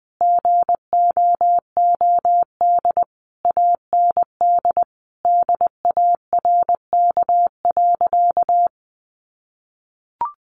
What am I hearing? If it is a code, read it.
Morse code, heard as a keyed tone, about 20 wpm, 700 Hz. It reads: GOOD AND DARK.
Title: Sentences from Top 600 Words 20wpm